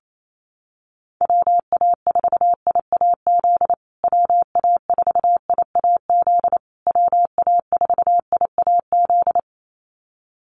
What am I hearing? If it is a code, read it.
Morse code, heard as a keyed tone, about 28 wpm, 700 Hz. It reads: WA4IAZ WA4IAZ WA4IAZ
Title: USA Call Signs Repeated 3x 28wpm